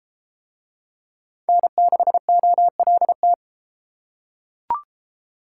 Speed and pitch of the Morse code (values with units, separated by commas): 29 wpm, 700 Hz